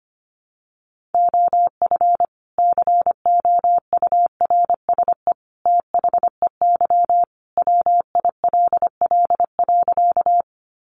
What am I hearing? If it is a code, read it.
Morse code, heard as a keyed tone, about 25 wpm, 700 Hz. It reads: OF COURSE THEY WILL.
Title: Sentences from Top 500 Words 25wpm